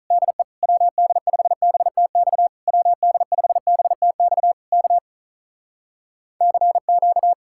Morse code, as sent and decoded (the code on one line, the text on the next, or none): -.. / .-- -.. ..... -... - -..- / .-- -.. ..... -... - -..- / -.- / -.-. --.-
DE WD5BTX WD5BTX K CQ